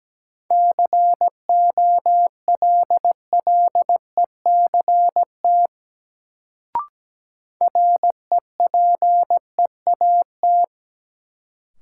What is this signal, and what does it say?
Morse code, heard as a keyed tone, about 17 wpm, 700 Hz. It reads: COLLECT REPEAT
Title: Sentences from Top 1600 Words - New Word Drill - 17wpm